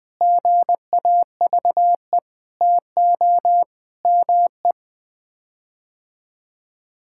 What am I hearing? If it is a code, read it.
Morse code, heard as a keyed tone, about 20 wpm, 700 Hz. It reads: GAVE TO ME
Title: Sets of 9 Words - 20wpm